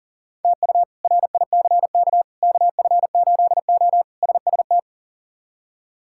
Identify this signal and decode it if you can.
Morse code, heard as a keyed tone, about 37 wpm, 700 Hz. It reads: TU RICK KF8O SST
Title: K1USN SST QSO Practice - 40wpm